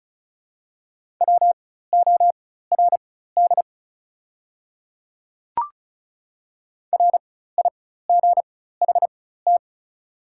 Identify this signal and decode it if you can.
Morse code, heard as a keyed tone, about 36 wpm, spaced as for 16 wpm, 700 Hz. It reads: WORD RIGHT
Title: 4x Character Spacing - Top 500 Words - 35wpm